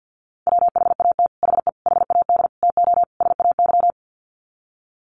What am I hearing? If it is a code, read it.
Morse code, heard as a keyed tone, about 50 wpm, 700 Hz. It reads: WHAT HE HAD TO SAY
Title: Sets of 5 Words Encore2 50wpm